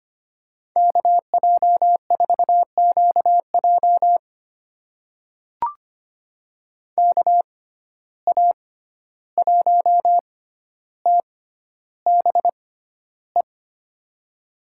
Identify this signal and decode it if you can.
Morse code, heard as a keyed tone, about 25 wpm, spaced as for 12 wpm, 700 Hz. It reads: KJ4QJ KA1TBE
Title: USA Call Signs - 6x Character Spacing - 25wpm